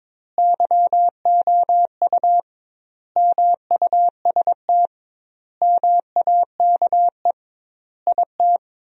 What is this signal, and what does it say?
Morse code, heard as a keyed tone, about 22 wpm, 700 Hz. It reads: YOU MUST MAKE IT
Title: 2x Word-Spacing - Sentences from Top 500 Words - 22wpm